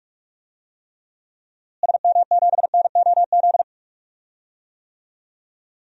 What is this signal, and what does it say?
Morse code, heard as a keyed tone, about 44 wpm, 700 Hz. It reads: SM7NOZ
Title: DXCC-100 Call Signs - 6 Characters - Repeated 3x - 45wpm